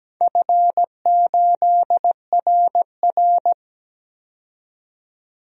Morse code, heard as a keyed tone, about 17 wpm, 700 Hz. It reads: F8RR
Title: DXCC-100 Call Signs - 4 Characters - 17wpm